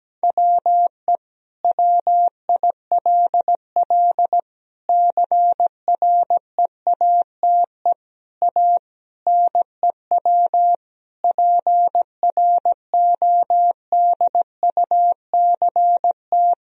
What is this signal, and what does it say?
Morse code, heard as a keyed tone, about 17 wpm, 700 Hz. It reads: WE WILL CREATE A NEW PRODUCT
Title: Sentences from Top 1000 Words - Review Part 3 - 17wpm